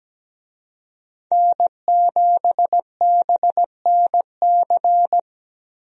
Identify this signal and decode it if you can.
Morse code, heard as a keyed tone, about 17 wpm, 700 Hz. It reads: N7BNC
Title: USA Call Signs - 5 Characters - Repeated 2x - 17wpm